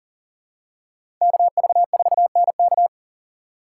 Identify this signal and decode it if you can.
Morse code, heard as a keyed tone, about 40 wpm, 700 Hz. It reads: KV4NK
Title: USA Call Signs - 2x Character Spacing - 40wpm